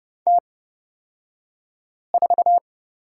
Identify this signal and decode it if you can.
Morse code, heard as a keyed tone, about 29 wpm, spaced as for 5 wpm, 700 Hz. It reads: T4
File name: Lesson 36d - BT - Two Characters 30wpm